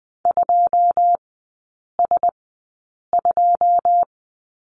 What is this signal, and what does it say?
Morse code, heard as a keyed tone, about 19 wpm, 700 Hz. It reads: 2 S 2
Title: 121 Lesson 20e - Number 3 - Three Characters 20wpm